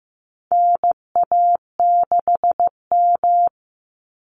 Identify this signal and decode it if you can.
Morse code, heard as a keyed tone, about 15 wpm, 700 Hz. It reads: NA6M
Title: Call Signs Repeated 3x - 15wpm